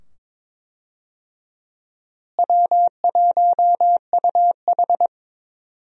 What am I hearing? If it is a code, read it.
Morse code, heard as a keyed tone, about 22 wpm, 700 Hz. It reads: W1UH
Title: USA Call Signs - 4 Characters - Repeated 2x - 22wpm